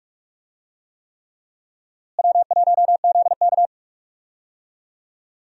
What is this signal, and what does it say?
Morse code, heard as a keyed tone, about 46 wpm, 700 Hz. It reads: W1ZK
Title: USA Call Signs - 4 Characters - Repeated 3x - 45wpm